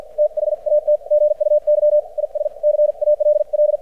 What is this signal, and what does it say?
Morse code, heard as a keyed tone, about 54 wpm, spaced as for 45 wpm, 600 Hz. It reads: THAT GUY IS CALL
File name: archived / CRSnet